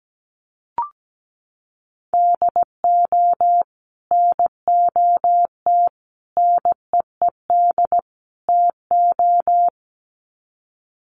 Morse code, heard as a keyed tone, about 17 wpm, 700 Hz. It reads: DO NOT NEED TO
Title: Sets of 4 Words 17wpm